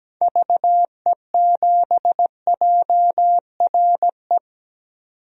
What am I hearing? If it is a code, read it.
Morse code, heard as a keyed tone, about 17 wpm, 700 Hz. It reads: VE7JRE
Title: DXCC-100 Call Signs - 6 Characters - 17wpm